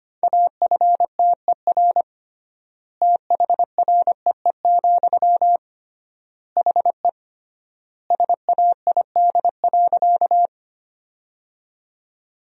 Morse code, heard as a keyed tone, about 25 wpm, 700 Hz. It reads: AFTER THREE, HE SAID.
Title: 3x Word-Spacing - Sentences from Top 200 Words - 25wpm